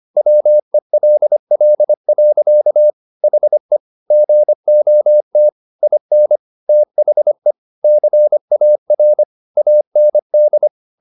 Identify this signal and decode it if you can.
Morse code, heard as a keyed tone, about 25 wpm, 600 Hz. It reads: WELL. HE GOT IN THE CAR AND